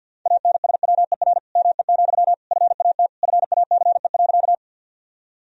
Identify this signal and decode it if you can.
Morse code, heard as a keyed tone, about 50 wpm, 700 Hz. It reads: ANSWER ME, FAT FACE.
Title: Sentences from Top 1000 Words - Review Part 4 - 50wpm